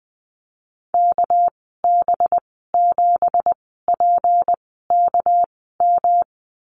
Morse code, heard as a keyed tone, about 20 wpm, spaced as for 14 wpm, 700 Hz. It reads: KB7PKM
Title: 200 Lesson 32g - Letter k - Call Signs 20wpm